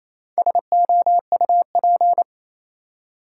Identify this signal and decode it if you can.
Morse code, heard as a keyed tone, about 28 wpm, 700 Hz. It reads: SOUP
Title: Four Letter Words - 28wpm